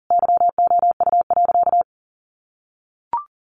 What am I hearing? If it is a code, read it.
Morse code, heard as a keyed tone, about 40 wpm, 700 Hz. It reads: YOU.
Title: Sentences from Top 1500 Words 40wpm